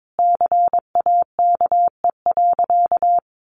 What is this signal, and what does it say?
Morse code, heard as a keyed tone, about 22 wpm, 700 Hz. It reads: CAKE.
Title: Additional Easy Phrases 22wpm